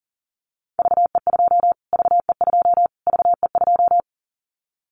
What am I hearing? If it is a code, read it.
Morse code, heard as a keyed tone, about 40 wpm, 700 Hz. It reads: VE2 VE2 VE2
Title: Callsign Prefixes for Top 25 DXCC - Repeated 3x - 40wpm